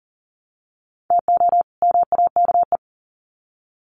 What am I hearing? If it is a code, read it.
Morse code, heard as a keyed tone, about 40 wpm, 700 Hz. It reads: TO MAKE
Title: Sets of 2 Words 40wpm